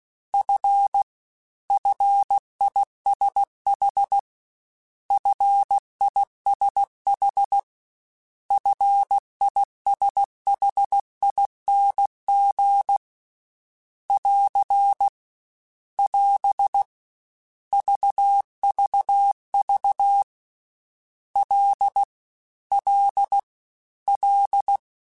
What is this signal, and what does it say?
Morse code, heard as a keyed tone, about 16 wpm, 800 Hz. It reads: F FISH FISH FISHING <AR> <AS> VVV L L L
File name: F and L / F and L mp3 / F and L Prefixes 3x2 Twice @15